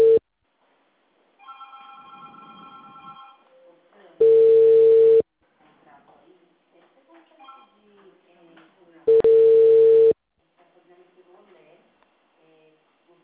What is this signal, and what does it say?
Ringback tone (China, 450 Hz), with speech.